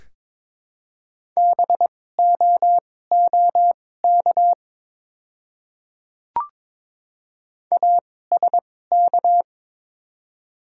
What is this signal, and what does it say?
Morse code, heard as a keyed tone, about 22 wpm, spaced as for 16 wpm, 700 Hz. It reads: BOOK ASK
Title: 2x Character Spacing - Top 300 Words - 22wpm